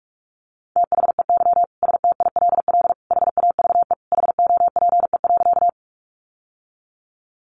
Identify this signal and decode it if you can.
Morse code, heard as a keyed tone, about 44 wpm, 700 Hz. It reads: THEY STILL HAVE HOPE.
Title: Sentences from Top 1000 Words 45wpm